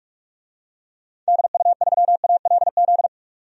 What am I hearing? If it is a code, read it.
Morse code, heard as a keyed tone, about 47 wpm, 700 Hz. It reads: DU3ALZ
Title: DXCC-100 Call Signs - 6 Characters - 45wpm